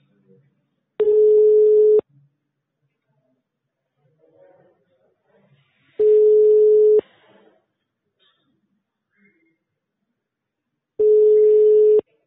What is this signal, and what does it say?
Ringback tone (Europe, 425 Hz), with speech.